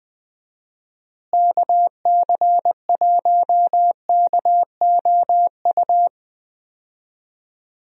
Morse code, heard as a keyed tone, about 20 wpm, 700 Hz. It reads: KC1KOU
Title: USA Call Signs - 6 Characters - Repeated 3x - 20wpm